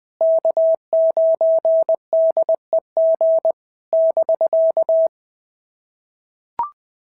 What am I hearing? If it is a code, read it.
Morse code, heard as a keyed tone, about 20 wpm, 650 Hz. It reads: K9DEG <BK>
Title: POTA QSO Practice - 20wpm